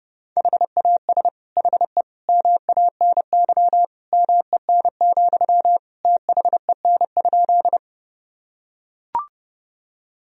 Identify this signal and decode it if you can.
Morse code, heard as a keyed tone, about 30 wpm, 700 Hz. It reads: HAS HE MANY MEN, THEN?
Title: Sentences from Top 1000 Words - Review Part 1 - 30wpm